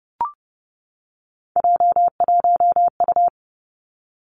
Morse code, heard as a keyed tone, about 31 wpm, 700 Hz. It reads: J1U
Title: Three Challenging Letter-Number Combinations 30wpm